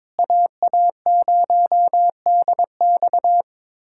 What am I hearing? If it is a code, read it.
Morse code, heard as a keyed tone, about 22 wpm, 700 Hz. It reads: AA0DX
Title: USA Callsign DE Callsign 22wpm